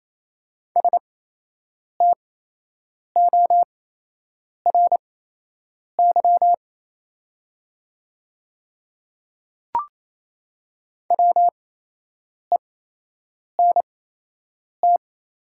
Morse code, heard as a keyed tone, about 27 wpm, spaced as for 8 wpm, 700 Hz. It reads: STORY WENT
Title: 8x Character Spacing - Top 300 Words - 28wpm